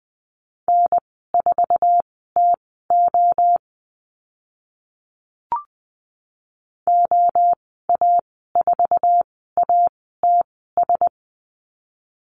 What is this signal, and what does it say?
Morse code, heard as a keyed tone, about 20 wpm, spaced as for 14 wpm, 700 Hz. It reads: N4TO OA4ATS
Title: 082 Lesson 14g - Number 2 - Call Signs 20wpm